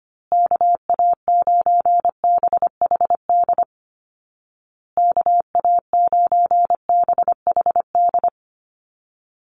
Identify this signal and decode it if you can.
Morse code, heard as a keyed tone, about 25 wpm, 700 Hz. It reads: KA9BHD KA9BHD
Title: Call Signs Repeated 2x - 25wpm